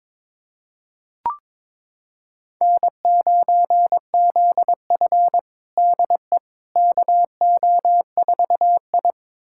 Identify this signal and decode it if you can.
Morse code, heard as a keyed tone, about 22 wpm, 700 Hz. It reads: N9ZF DE KO4I
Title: USA Callsign DE Callsign - 4 Characters - 22wpm